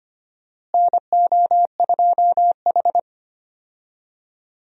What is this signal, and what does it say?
Morse code, heard as a keyed tone, about 25 wpm, 700 Hz. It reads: NO2H
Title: USA Call Signs - 4 Characters - Repeated 2x - 25wpm